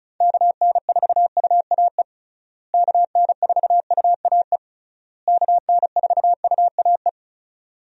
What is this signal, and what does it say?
Morse code, heard as a keyed tone, about 36 wpm, 700 Hz. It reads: KN4UAE KN4UAE KN4UAE
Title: USA Call Signs - 6 Characters - Repeated 3x - 35wpm